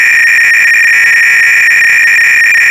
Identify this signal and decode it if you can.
300-baud modem data.